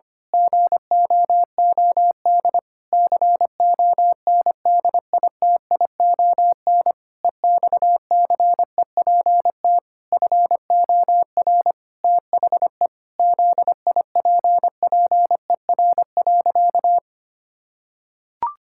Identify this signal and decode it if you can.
Morse code, heard as a keyed tone, about 25 wpm, 700 Hz. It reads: GOOD CONDITION EXCEPT FOR THE ZIPPER.
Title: Pangram Challenge v2 - 25wpm